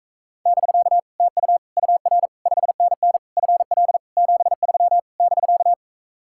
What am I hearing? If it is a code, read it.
Morse code, heard as a keyed tone, about 44 wpm, 700 Hz. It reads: <BK> TU UR 5NN FL 73 <BK>